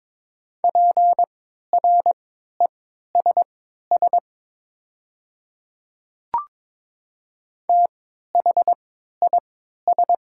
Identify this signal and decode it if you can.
Morse code, heard as a keyed tone, about 22 wpm, spaced as for 13 wpm, 700 Hz. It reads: PRESS THIS
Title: 3x Character Spacing - Top 300 Words - 22wpm